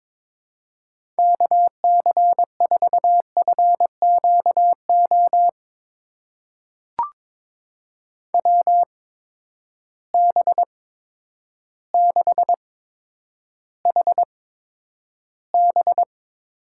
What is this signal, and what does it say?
Morse code, heard as a keyed tone, about 22 wpm, spaced as for 10 wpm, 700 Hz. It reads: KC4FQO WB6HB
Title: USA Call Signs - 6 Characters - 8x Character Spacing - 22wpm